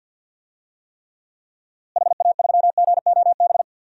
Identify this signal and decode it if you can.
Morse code, heard as a keyed tone, about 47 wpm, 700 Hz. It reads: HA3GOB
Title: DXCC-100 Call Signs - 6 Characters - Repeated 3x - 50wpm